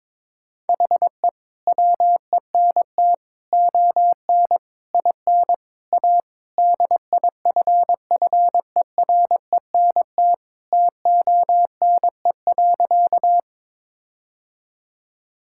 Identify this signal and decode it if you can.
Morse code, heard as a keyed tone, about 22 wpm, 700 Hz. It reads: HE WENT ON IN A DIFFERENT TONE.
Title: Sentences from Top 1600 Words 22wpm